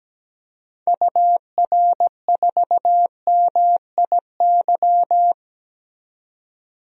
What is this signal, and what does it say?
Morse code, heard as a keyed tone, about 17 wpm, 700 Hz. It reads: UR4MIY
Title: DXCC-100 Call Signs - 6 Characters - 17wpm